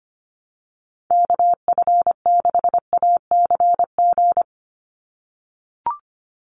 Morse code, heard as a keyed tone, about 25 wpm, 700 Hz. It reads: KF6ACG
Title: USA Call Signs 25wpm